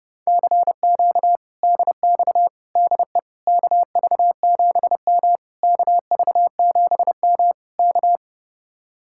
Morse code, heard as a keyed tone, about 32 wpm, 700 Hz. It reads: CQ DX DE KV7M KV7M K